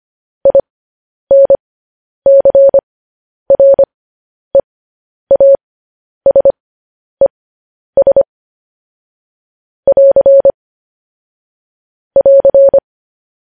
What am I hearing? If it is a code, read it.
Morse code, heard as a keyed tone, about 25 wpm, spaced as for 10 wpm, 550 Hz. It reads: INCREASES <AR> <AR>
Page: news-headlines-in-morse-code-at-10-wpm